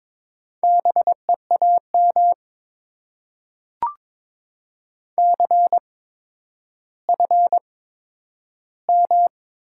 Morse code, heard as a keyed tone, about 22 wpm, spaced as for 11 wpm, 700 Hz. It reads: BEAM CFM
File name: Top 90 QSO Elements - 8x Character Spacing - 22wpm